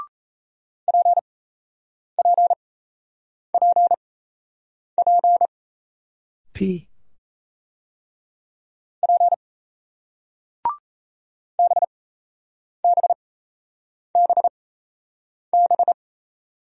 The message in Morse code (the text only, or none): PPPP P BBBB